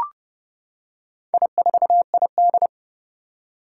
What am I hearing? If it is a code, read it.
Morse code, heard as a keyed tone, about 30 wpm, 700 Hz. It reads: I4ID
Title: Four Challenging Letter-Number Combinations 30wpm